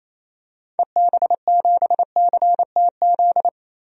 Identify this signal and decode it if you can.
Morse code, heard as a keyed tone, about 28 wpm, 700 Hz. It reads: EB7CTZ
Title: DXCC-100 Call Signs - 6 Characters - Repeated 3x - 28wpm